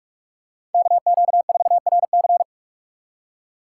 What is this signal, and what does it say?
Morse code, heard as a keyed tone, about 46 wpm, 700 Hz. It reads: KQ4RC
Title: USA Call Signs - 5 Characters - 2x Character Spacing - 45wpm